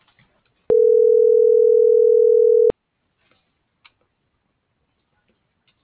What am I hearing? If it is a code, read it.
Ringback tone (North America), 440 + 480 Hz.